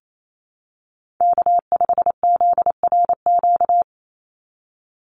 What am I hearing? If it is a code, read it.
Morse code, heard as a keyed tone, about 28 wpm, 700 Hz. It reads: K5ZRQ
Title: USA Call Signs 28wpm